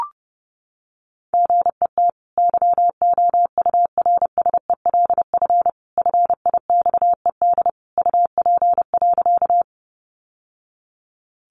Morse code, heard as a keyed tone, about 30 wpm, 700 Hz. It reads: GET YOURSELF FIXED UP.
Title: Sentences from Top 1100 Words 30wpm